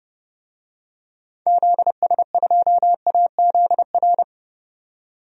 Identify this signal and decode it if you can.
Morse code, heard as a keyed tone, about 30 wpm, 700 Hz. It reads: ZS2AZR
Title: DXCC-100 Call Signs - 6 Characters - 30wpm